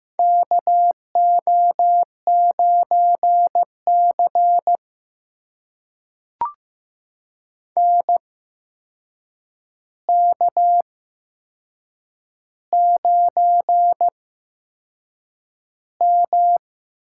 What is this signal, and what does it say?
Morse code, heard as a keyed tone, about 15 wpm, spaced as for 6 wpm, 700 Hz. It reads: KO9CN K9M
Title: USA Call Signs - 8x Character Spacing - 15wpm